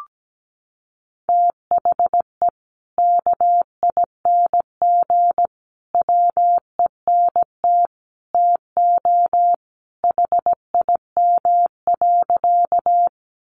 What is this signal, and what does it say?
Morse code, heard as a keyed tone, about 17 wpm, 700 Hz. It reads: THE KING WENT TO HIM.
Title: Sentences from Top 400 Words 17wpm